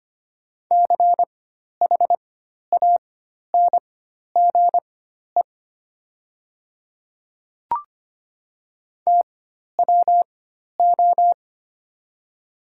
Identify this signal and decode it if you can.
Morse code, heard as a keyed tone, about 25 wpm, spaced as for 12 wpm, 700 Hz. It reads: CHANGE TWO
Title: 4x Character Spacing - Top 300 Words - 25wpm